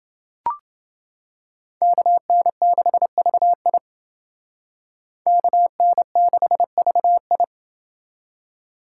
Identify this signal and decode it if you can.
Morse code, heard as a keyed tone, about 29 wpm, 700 Hz. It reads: KN6VI KN6VI